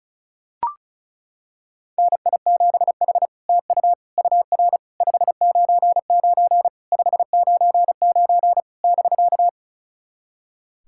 Morse code, heard as a keyed tone, about 36 wpm, 700 Hz. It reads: NI7H TU UR 599 599 <BK>
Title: SOTA QSO Practice 35wpm